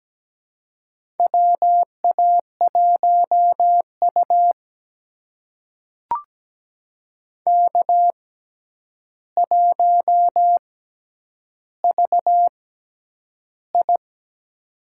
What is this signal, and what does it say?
Morse code, heard as a keyed tone, about 17 wpm, spaced as for 9 wpm, 700 Hz. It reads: WA1U K1VI
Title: USA Call Signs - 4 Characters - 6x Character Spacing - 17wpm